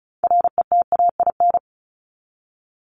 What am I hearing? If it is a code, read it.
Morse code, heard as a keyed tone, about 34 wpm, 700 Hz. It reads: RETAIN
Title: Top 1751-2000 Words 35wpm